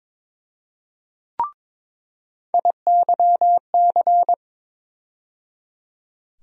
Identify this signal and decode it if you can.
Morse code, heard as a keyed tone, about 22 wpm, 700 Hz. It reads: IYC
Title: USA Call Sign Postfixes - 3 Letters 22wpm